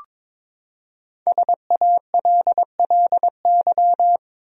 Morse code, heard as a keyed tone, about 22 wpm, 700 Hz. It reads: SALLY
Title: USA Common Names - Max 5 Letters - 22wpm